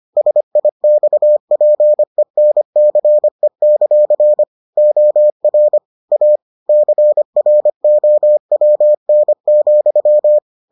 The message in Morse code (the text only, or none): SIXPENCE; OR A CROWN,